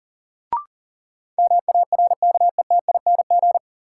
Morse code, heard as a keyed tone, about 40 wpm, 700 Hz. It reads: MARKETING
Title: Sentences from Top 1000 Words - Review Part 5 - New Word Drill - 40wpm